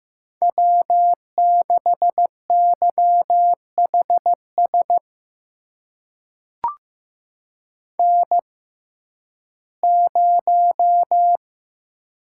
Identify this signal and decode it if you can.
Morse code, heard as a keyed tone, about 15 wpm, spaced as for 11 wpm, 700 Hz. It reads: W6YHS N0
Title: USA Call Signs - 5 Characters - 6x Character Spacing - 15wpm